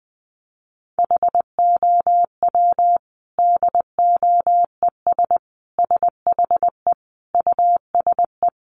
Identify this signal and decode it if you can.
Morse code, heard as a keyed tone, about 20 wpm, 700 Hz. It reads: HOW DOES SHE USE
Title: Sentences from Top 200 Words 20wpm